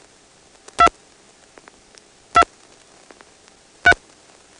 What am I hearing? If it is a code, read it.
DTMF, heard as telephone touch tones, tones of about 70 ms. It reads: 2 2 3
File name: phonetrips / PrinceWmCoVA5HQ